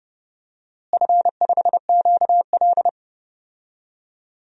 Morse code, heard as a keyed tone, about 31 wpm, 700 Hz. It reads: F5QL
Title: DXCC-100 Call Signs - 4 Characters - Repeated 3x - 30wpm